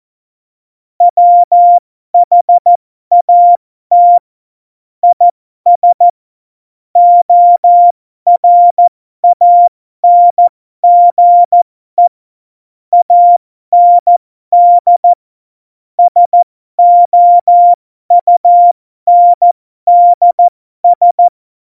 Morse code, heard as a keyed tone, about 14 wpm, spaced as for 12 wpm, 700 Hz. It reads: WHAT IS ORANGE AND SOUNDS
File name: Morsekurs / Witze